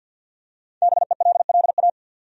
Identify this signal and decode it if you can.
Morse code, heard as a keyed tone, about 53 wpm, 700 Hz. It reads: BELLA
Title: USA Common Names - 50wpm